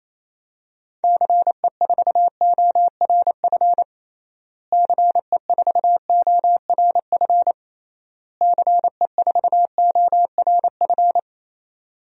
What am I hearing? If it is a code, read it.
Morse code, heard as a keyed tone, about 28 wpm, 700 Hz. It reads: CE4ORF CE4ORF CE4ORF
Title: DXCC-100 Call Signs - 6 Characters - Repeated 3x - 28wpm